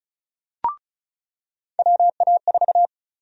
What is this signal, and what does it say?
Morse code, heard as a keyed tone, about 36 wpm, 700 Hz. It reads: WA4
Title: USA Call Sign Prefixes - 35wpm